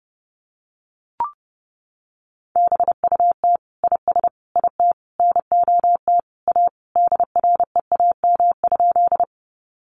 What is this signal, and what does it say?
Morse code, heard as a keyed tone, about 30 wpm, 700 Hz. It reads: BUT IS IT NOT A DREAM?
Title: Sentences from Top 1500 Words 30wpm